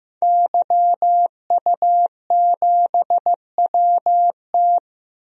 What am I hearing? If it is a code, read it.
Morse code, heard as a keyed tone, about 15 wpm, 700 Hz. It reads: YU7WT
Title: DXCC-100 Call Signs - 5 Characters - Repeated 2x - 15wpm